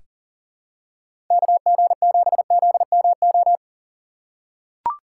Morse code, heard as a keyed tone, about 40 wpm, 700 Hz. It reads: KG7ZMO
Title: USA Call Signs - 6 Characters - 4x Character Spacing - 40wpm